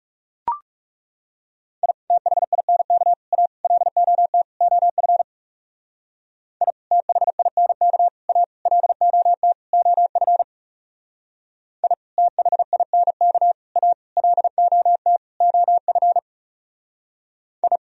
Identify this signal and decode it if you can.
Morse code, heard as a keyed tone, about 36 wpm, 700 Hz. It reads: I THINK A LOT OF I THINK A LOT OF I THINK A LOT OF I
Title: Speed-Racing - Sets of 5 Words 30wpm